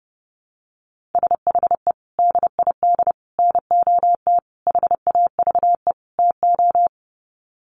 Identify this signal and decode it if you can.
Morse code, heard as a keyed tone, about 30 wpm, 700 Hz. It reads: SHE DID NOT HAVE TO
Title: Sets of 5 Words Encore2 30wpm